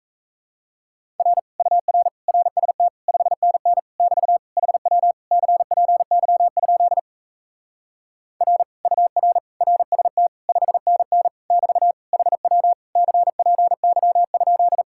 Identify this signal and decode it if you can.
Morse code, heard as a keyed tone, about 37 wpm, 700 Hz. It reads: R UR RST 5NN <BT> HW CPY? R UR RST 5NN <BT> HW CPY?